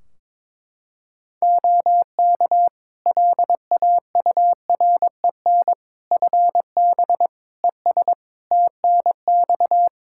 Morse code, heard as a keyed tone, about 22 wpm, 700 Hz. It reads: OK LAUREN FB ES TNX
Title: Practice QSO - Exchange 2 - Info - 22wpm